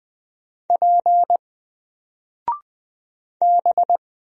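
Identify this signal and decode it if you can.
Morse code, heard as a keyed tone, about 20 wpm, 700 Hz. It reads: P B
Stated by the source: Single letter and number Morse drill at 20 wpm, from N4LCD.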